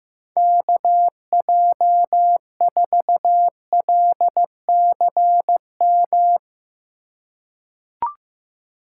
Morse code, heard as a keyed tone, about 15 wpm, 700 Hz. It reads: KJ4LCM
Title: USA Call Signs - 6 Characters - 3x Character Spacing - 15wpm